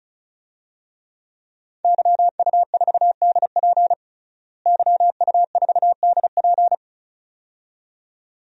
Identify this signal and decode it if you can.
Morse code, heard as a keyed tone, about 36 wpm, 700 Hz. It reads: YU4DP YU4DP
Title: DXCC-100 Call Signs - 5 Characters - Repeated 2x - 35wpm